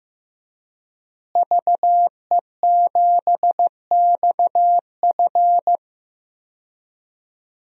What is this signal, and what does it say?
Morse code, heard as a keyed tone, about 15 wpm, 700 Hz. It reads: VE7XF
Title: DXCC-100 Call Signs - 5 Characters - Repeated 2x - 15wpm